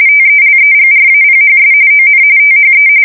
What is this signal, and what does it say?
RTTY, about 45.45 baud. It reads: TELEGRAPHS ON THE
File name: fsk170 / morkrum